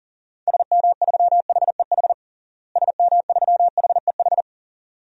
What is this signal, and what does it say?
Morse code, heard as a keyed tone, about 40 wpm, 700 Hz. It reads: SM3HEH SM3HEH
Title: DXCC-100 Call Signs - 6 Characters - Repeated 2x - 40wpm